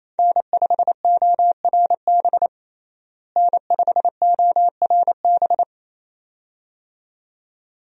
Morse code, heard as a keyed tone, about 28 wpm, 700 Hz. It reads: N5ORB N5ORB
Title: USA Call Signs - 5 Characters - Repeated 2x - 28wpm